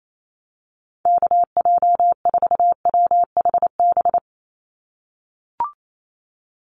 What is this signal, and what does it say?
Morse code, heard as a keyed tone, about 28 wpm, 700 Hz. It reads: KJ4WHB
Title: USA Call Signs 28wpm